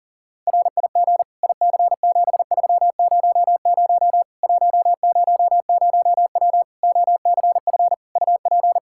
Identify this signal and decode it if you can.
Morse code, heard as a keyed tone, about 40 wpm, 700 Hz. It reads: RIG IC7300 100W OCF UP
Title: Practice QSO - Exchange 2 - Info - 40wpm